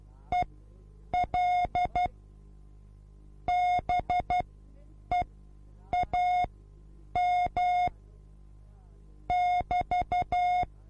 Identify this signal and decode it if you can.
Morse code, heard as a keyed tone, about 12 wpm, spaced as for 8 wpm, 720 Hz. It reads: EL BEAM <BT>
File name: repos / Morsekurs / DJ4UF Morsen24